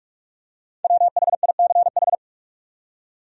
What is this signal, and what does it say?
Morse code, heard as a keyed tone, about 47 wpm, 700 Hz. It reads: WHICH
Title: Top 100 Words 45wpm